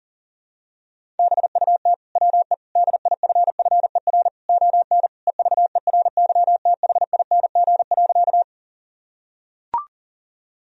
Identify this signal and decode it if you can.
Morse code, heard as a keyed tone, about 40 wpm, 700 Hz. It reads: BUT WE DIFFER ON EVERYTHING.
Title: Sentences from Top 1000 Words - Review Part 1 - 40wpm